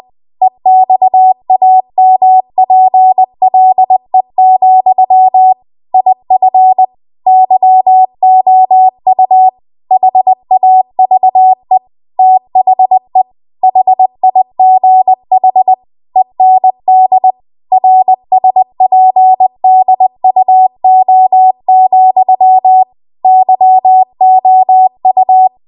Morse code, heard as a keyed tone, about 20 wpm, 750 Hz. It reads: EXAMPLE, IF YOU HAVE THE HIGH END RSPDUO, YOU
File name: Archive / 20 WPM / 20WPM